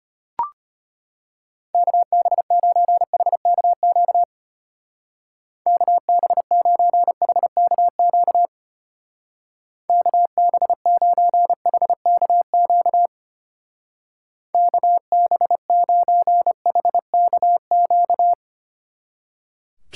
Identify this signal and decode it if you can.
Morse code, heard as a keyed tone, about 31 wpm, 700 Hz. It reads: KB9HKQ KB9HKQ KB9HKQ KB9HKQ
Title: Speed Racing - USA 2x3 Call Signs 25wpm